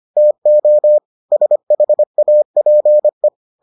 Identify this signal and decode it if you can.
Morse code, heard as a keyed tone, about 25 wpm, 600 Hz. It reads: TO SHAPE